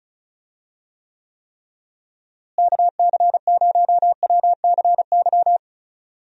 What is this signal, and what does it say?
Morse code, heard as a keyed tone, about 34 wpm, 700 Hz. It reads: KC0WCY
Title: USA Call Signs - 6 Characters - Repeated 3x - 35wpm